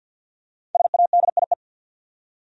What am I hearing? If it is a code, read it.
Morse code, heard as a keyed tone, about 49 wpm, 700 Hz. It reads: SADIE
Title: USA Common Names - 50wpm